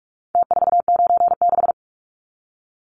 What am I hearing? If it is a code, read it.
Morse code, heard as a keyed tone, about 44 wpm, 700 Hz. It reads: T496
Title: Numbers - Contest Style - 4-Digit 45wpm